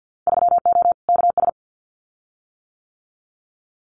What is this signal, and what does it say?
Morse code, heard as a keyed tone, about 50 wpm, 700 Hz. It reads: 3O KS
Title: Winter Field Day Exchanges Encore 50wpm